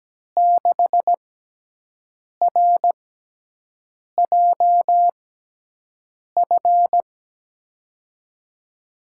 Morse code, heard as a keyed tone, about 17 wpm, spaced as for 6 wpm, 700 Hz. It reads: 6RJF
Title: USA Call Signs - 6x Character Spacing - 17wpm